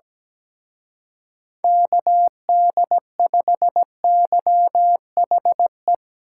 Morse code, heard as a keyed tone, about 19 wpm, 700 Hz. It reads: KD5YHE
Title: Speed Racing - USA Call Signs 17wpm